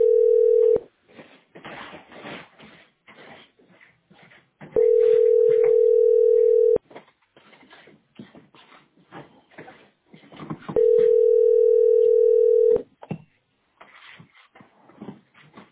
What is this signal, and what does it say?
Ringback tone (North America), 440 + 480 Hz.